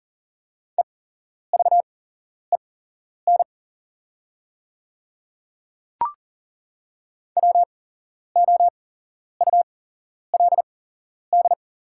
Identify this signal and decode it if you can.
Morse code, heard as a keyed tone, about 40 wpm, spaced as for 11 wpm, 700 Hz. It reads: EVEN WOULD
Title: 8x Character Spacing - Top 200 Words - 40wpm